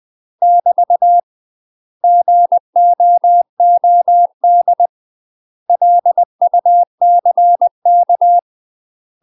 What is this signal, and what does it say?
Morse code, heard as a keyed tone, about 20 wpm, 700 Hz. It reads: <BT> GOOD LUCK
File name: UK MASTER QSOs TRAINING / 20WPM / UK MASTER QSOs 0045